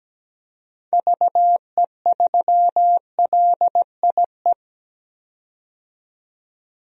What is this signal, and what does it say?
Morse code, heard as a keyed tone, about 17 wpm, 700 Hz. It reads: VE3LIE
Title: DXCC-100 Call Signs - 6 Characters - 17wpm